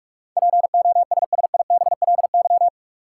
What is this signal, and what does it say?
Morse code, heard as a keyed tone, about 46 wpm, 700 Hz. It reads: POSSIBLY
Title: Sentences from Top 1300 Words - New Word Drill - 45wpm